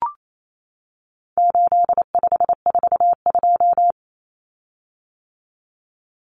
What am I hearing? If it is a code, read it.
Morse code, heard as a keyed tone, about 28 wpm, 700 Hz. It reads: 8542
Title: Common Cut Numbers - 4-Digit 28wpm